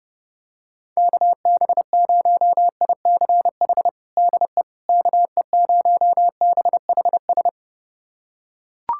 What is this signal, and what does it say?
Morse code, heard as a keyed tone, about 32 wpm, 700 Hz. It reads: KB0ICH DE KE0BHS
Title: USA Callsign DE Callsign - 6 Characters - 30wpm